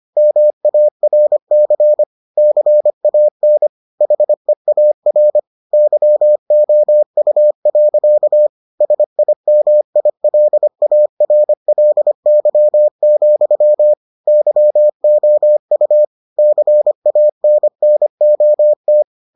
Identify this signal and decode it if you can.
Morse code, heard as a keyed tone, about 25 wpm, 600 Hz. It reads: MARC CAN HEAR YOU. SIMILARLY, YOU CANNOT